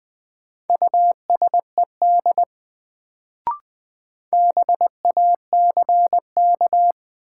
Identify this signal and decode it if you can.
Morse code, heard as a keyed tone, about 21 wpm, 700 Hz. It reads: USED BACK
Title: Sets of 2 Words - New Word Drill - 20wpm